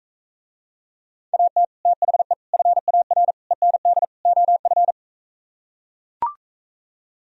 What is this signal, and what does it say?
Morse code, heard as a keyed tone, about 43 wpm, 700 Hz. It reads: AT THE FAR END OF